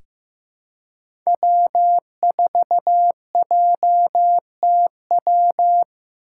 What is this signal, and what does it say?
Morse code, heard as a keyed tone, about 15 wpm, 700 Hz. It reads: W4JTW
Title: USA Call Signs - 5 Characters - 4x Character Spacing - 15wpm